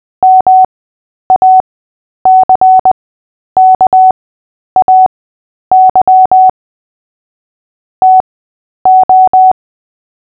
Morse code, heard as a keyed tone, about 20 wpm, spaced as for 10 wpm, 750 Hz. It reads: MACKAY TO
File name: Short-Story-5-10wpm